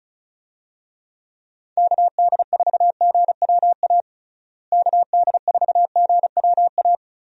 Morse code, heard as a keyed tone, about 36 wpm, 700 Hz. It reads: KD4GWA KD4GWA
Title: USA Call Signs - 6 Characters - Repeated 2x - 35wpm